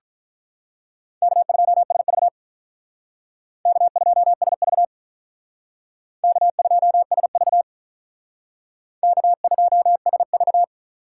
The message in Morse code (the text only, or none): K2SV K2SV K2SV K2SV